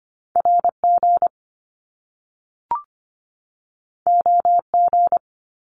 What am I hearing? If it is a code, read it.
Morse code, heard as a keyed tone, about 25 wpm, 700 Hz. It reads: RG OG
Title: Top Two Letter Combinations in Words 25wpm